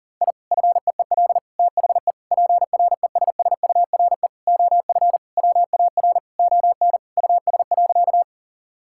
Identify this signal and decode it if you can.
Morse code, heard as a keyed tone, about 40 wpm, 700 Hz. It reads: I FEEL THE PRESSURE OF WAR ON US.
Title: Sentences from Top 1000 Words - Review Part 6 - 40wpm